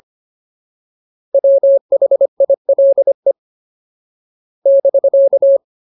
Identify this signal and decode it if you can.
Morse code, heard as a keyed tone, about 25 wpm, 550 Hz. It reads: WHILE <BK>
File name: Ultimate Word List / 25WPM Ultimate